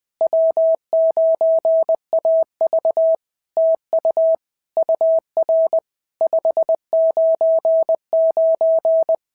Morse code, heard as a keyed tone, about 20 wpm, 650 Hz. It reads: W9AV TU UR 599
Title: POTA QSO Practice - 20wpm